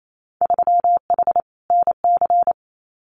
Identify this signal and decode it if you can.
Morse code, heard as a keyed tone, about 28 wpm, 700 Hz. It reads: 3H NC
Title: Winter Field Day Exchanges 28wpm